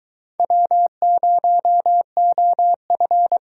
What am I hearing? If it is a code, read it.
Morse code, heard as a keyed tone, about 23 wpm, 700 Hz. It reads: W0OF